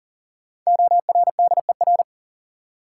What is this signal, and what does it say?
Morse code, heard as a keyed tone, about 39 wpm, 700 Hz. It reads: ORDER
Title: Sets of 2 Words - New Word Drill - 40wpm